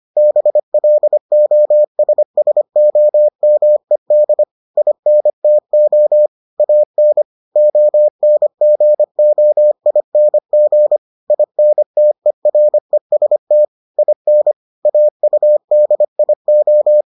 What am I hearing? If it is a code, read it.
Morse code, heard as a keyed tone, about 25 wpm, 600 Hz. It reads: BLOSSOMED INTO AN ONGOING INTEREST IN AUDIO